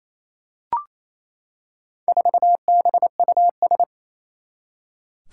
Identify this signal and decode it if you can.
Morse code, heard as a keyed tone, about 27 wpm, 700 Hz. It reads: 4BUS
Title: Four Challenging Letter-Number Combinations 28wpm